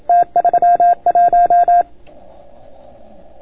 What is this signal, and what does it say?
Morse code, heard as a keyed tone, about 29 wpm, 690 Hz. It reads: T31